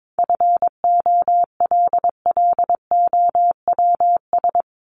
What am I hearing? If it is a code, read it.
Morse code, heard as a keyed tone, about 22 wpm, 700 Hz. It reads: FOLLOWS
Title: ES:S Suffix - Top 100 Words with Base Words 22wpm